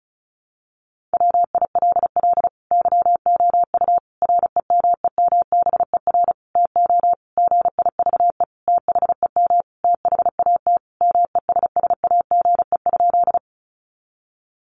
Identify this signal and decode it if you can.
Morse code, heard as a keyed tone, about 34 wpm, 700 Hz. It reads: WILL YOU REMEMBER TO GIVE THEM THAT MESSAGE?
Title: Sentences from Top 1100 Words 35wpm